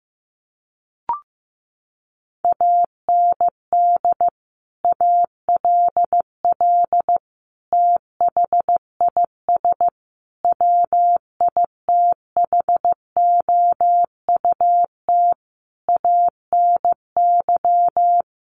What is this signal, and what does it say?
Morse code, heard as a keyed tone, about 15 wpm, 700 Hz. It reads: AND ALL THIS WITHOUT ANY
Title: Sentences from Top 1200 Words 15wpm